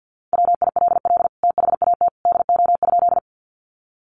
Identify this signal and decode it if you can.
Morse code, heard as a keyed tone, about 50 wpm, 700 Hz. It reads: WILL THAT DO?
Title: Sentences from Top 100 Words 50wpm